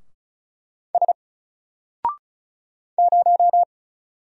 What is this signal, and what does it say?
Morse code, heard as a keyed tone, about 35 wpm, 700 Hz. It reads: S 0
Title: Single Character (Letter:Number:Punctuation:Prosign) 35wpm